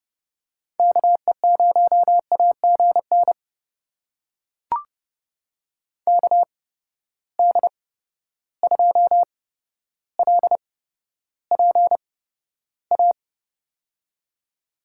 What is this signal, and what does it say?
Morse code, heard as a keyed tone, about 30 wpm, spaced as for 13 wpm, 700 Hz. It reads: KE0AGN KD2LPA
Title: USA Call Signs - 6 Characters - 8x Character Spacing - 30wpm